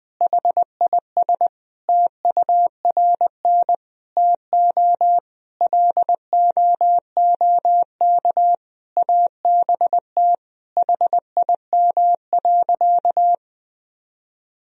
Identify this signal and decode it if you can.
Morse code, heard as a keyed tone, about 20 wpm, 700 Hz. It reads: HIS TURN TO LOOK ABT HIM.
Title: Sentences from Top 200 Words and Abbreviations 20wpm